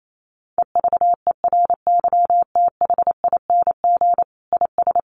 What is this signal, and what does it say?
Morse code, heard as a keyed tone, about 28 wpm, 700 Hz. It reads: EVERYTHING IS
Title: Sentences from Top 1400 Words 28wpm